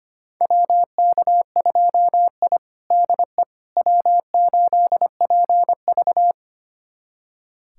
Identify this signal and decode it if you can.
Morse code, heard as a keyed tone, about 25 wpm, 700 Hz. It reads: WK2I DE W8PV
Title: USA Callsign DE Callsign - 4 Characters - 25wpm